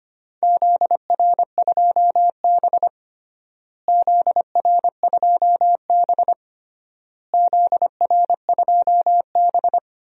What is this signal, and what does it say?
Morse code, heard as a keyed tone, about 25 wpm, 700 Hz. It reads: ZR2B ZR2B ZR2B
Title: DXCC-100 Call Signs - 4 Characters - Repeated 3x - 25wpm